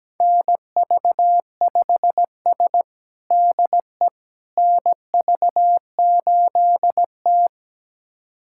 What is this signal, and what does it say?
Morse code, heard as a keyed tone, about 17 wpm, 700 Hz. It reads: NV5S DE NV8T
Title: USA Callsign DE Callsign - 4 Characters - 17wpm